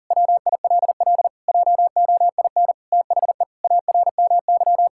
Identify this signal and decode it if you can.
Morse code, heard as a keyed tone, about 40 wpm, 700 Hz. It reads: WILL JOIN THE ARMY
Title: Sentences from Top 1000 Words - Review Part 6 - 40wpm